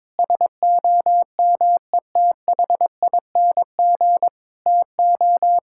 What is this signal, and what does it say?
Morse code, heard as a keyed tone, about 22 wpm, 700 Hz. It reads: SOMETHING TO
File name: practice / 22wpm / Sentences with 2-Word Phrases - 22wpm